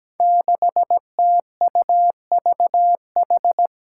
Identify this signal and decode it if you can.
Morse code, heard as a keyed tone, about 17 wpm, 700 Hz. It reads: 6TUVH
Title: Five Challenging Letter-Number Combinations 17wpm